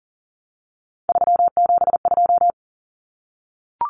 Morse code, heard as a keyed tone, about 40 wpm, 700 Hz. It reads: 372
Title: Common Cut Numbers - 3-Digit 40wpm